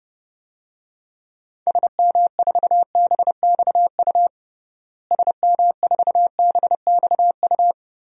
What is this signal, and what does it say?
Morse code, heard as a keyed tone, about 30 wpm, 700 Hz. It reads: SM4BXU SM4BXU
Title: DXCC-100 Call Signs - 6 Characters - Repeated 3x - 30wpm